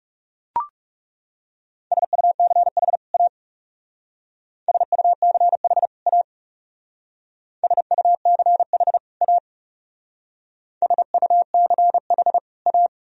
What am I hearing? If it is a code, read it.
Morse code, heard as a keyed tone, about 40 wpm, 700 Hz. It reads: SUCH A SUCH A SUCH A SUCH A